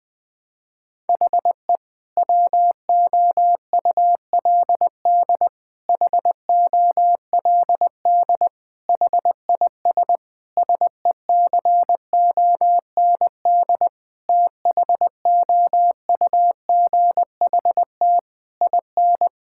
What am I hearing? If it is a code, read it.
Morse code, heard as a keyed tone, about 20 wpm, 700 Hz. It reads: HE WOULD HOLD HIS SECOND THOUGHT IN
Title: Sentences from Top 1000 Words - Review Part 6 - 20wpm